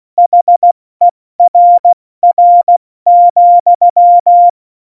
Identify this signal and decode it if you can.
Morse code, heard as a keyed tone, about 16 wpm, 700 Hz. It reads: HERR,
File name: Morsekurs / Witze / Witz 16. April 2019, 70 F80